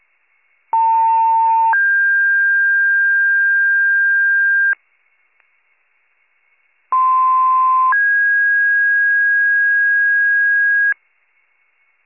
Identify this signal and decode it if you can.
Two-tone page, heard as a one-second tone, then a three-second tone, about 900 then 1600 Hz, about 1040 then 1650 Hz. It is heard from 0.7 to 10.9 s.